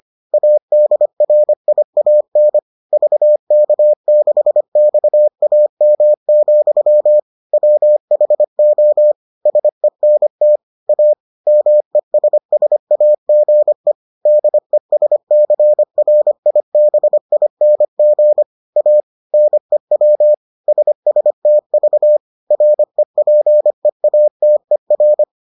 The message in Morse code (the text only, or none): ADRIAN VK6XAM, WHO SENT A MESSAGE DESCRIBING A NEW SSTV REPEATER